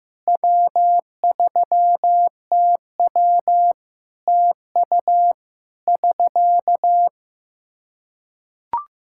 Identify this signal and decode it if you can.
Morse code, heard as a keyed tone, about 16 wpm, 700 Hz. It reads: W3TW TU <SK>
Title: Practice QSO - Exchange 4 - Ending - 15wpm